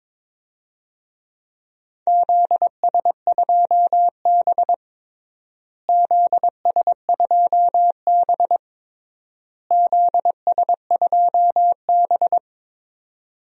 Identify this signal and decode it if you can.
Morse code, heard as a keyed tone, about 22 wpm, 700 Hz. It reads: ZS2B ZS2B ZS2B
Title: DXCC-100 Call Signs - 4 Characters - Repeated 3x - 22wpm